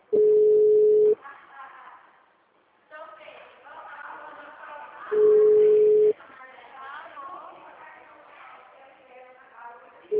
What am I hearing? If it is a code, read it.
Ringback tone (Europe, 425 Hz), with speech.